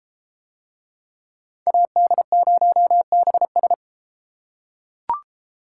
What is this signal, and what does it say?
Morse code, heard as a keyed tone, about 30 wpm, 700 Hz. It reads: AD0BS